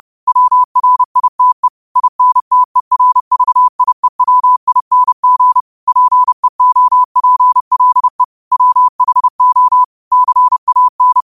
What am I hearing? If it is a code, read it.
Morse code, heard as a keyed tone, about 30 wpm, 1000 Hz. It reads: WRITE INTERVIEWING PEOPLE WHO CAN
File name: qotdfiles / 30wpm